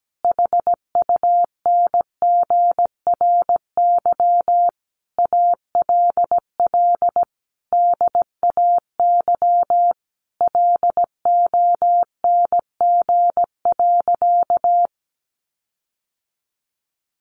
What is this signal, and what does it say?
Morse code, heard as a keyed tone, about 17 wpm, 700 Hz. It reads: HUNGRY ALL DAY LONG.
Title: Sentences from Top 1400 Words 17wpm